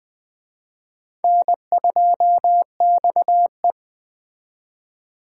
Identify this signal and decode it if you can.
Morse code, heard as a keyed tone, about 20 wpm, 700 Hz. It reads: N2XE
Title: Speed Racing - USA 1x2 Call Signs 17wpm